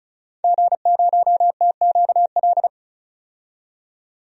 Morse code, heard as a keyed tone, about 36 wpm, 700 Hz. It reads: G0TQL
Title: DXCC-100 Call Signs - 5 Characters - 35wpm